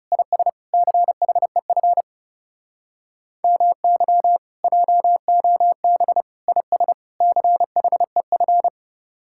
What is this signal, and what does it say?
Morse code, heard as a keyed tone, about 40 wpm, 700 Hz. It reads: IS CHEF MY JOB IS CHEF